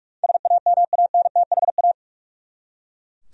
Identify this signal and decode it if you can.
Morse code, heard as a keyed tone, about 41 wpm, 700 Hz. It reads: SAMANTHA
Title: USA Common Names - 45wpm